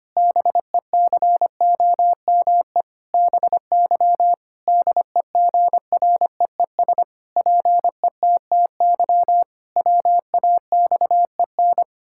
Morse code, heard as a keyed tone, about 25 wpm, 700 Hz. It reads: BECOME BY DEGREES PETTY WAXEN
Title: Pangram Challenge v2 - 25wpm